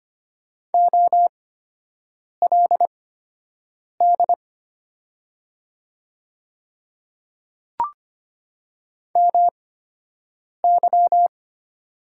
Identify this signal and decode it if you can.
Morse code, heard as a keyed tone, about 25 wpm, spaced as for 7 wpm, 700 Hz. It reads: OLD MY
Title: 8x Character Spacing - Top 200 Words - 25wpm